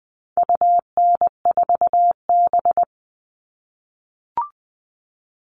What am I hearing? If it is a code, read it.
Morse code, heard as a keyed tone, about 20 wpm, 700 Hz. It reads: UN4B
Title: All Cut Numbers - 4-Digit 20wpm